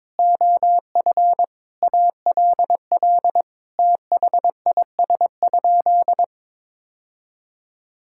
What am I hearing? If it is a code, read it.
Morse code, heard as a keyed tone, about 22 wpm, 700 Hz. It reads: OF ALL THIS?
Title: Sentences from Top 1000 Words - Review Part 6 - 22wpm